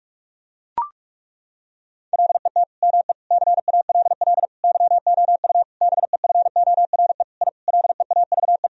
Morse code, heard as a keyed tone, about 41 wpm, 700 Hz. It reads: LET ME CALL YOU BEFORE I LEAVE
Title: Sentences from Top 1000 Words - Review Part 2 - 45wpm